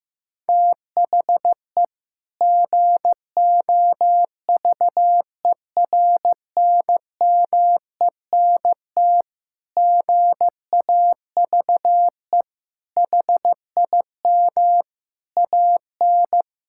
Morse code, heard as a keyed tone, about 15 wpm, 700 Hz. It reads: THE GOVERNMENT GAVE HIM AN
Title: Sentences from Top 1000 Words - Review Part 5 - 15wpm